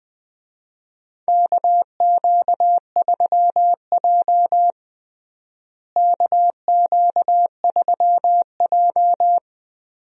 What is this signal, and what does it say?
Morse code, heard as a keyed tone, about 20 wpm, 700 Hz. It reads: KQ3J KQ3J
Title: USA Call Signs - 4 Characters - Repeated 3x - 20wpm